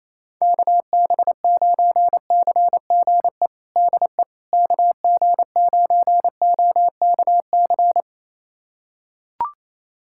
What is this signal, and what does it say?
Morse code, heard as a keyed tone, about 28 wpm, 700 Hz. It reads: KB9CGE DE KG9OKC
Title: USA Callsign DE Callsign - 6 Characters - 28wpm